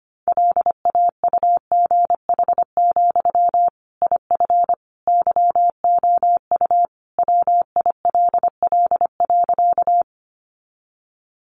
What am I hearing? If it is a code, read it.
Morse code, heard as a keyed tone, about 25 wpm, 700 Hz. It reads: LAUGH, IF YOU WILL.
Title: Sentences from Top 500 Words 25wpm